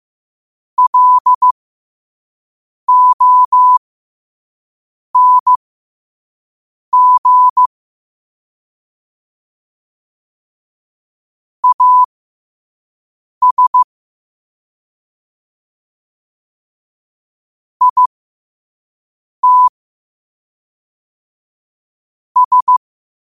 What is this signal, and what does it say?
Morse code, heard as a keyed tone, about 15 wpm, spaced as for 5 wpm, 1000 Hz. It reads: LONG AS IT S
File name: qotdfiles / 5wpm